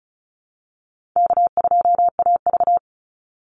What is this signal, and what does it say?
Morse code, heard as a keyed tone, about 34 wpm, 700 Hz. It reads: K2AV
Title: Call Signs - 35wpm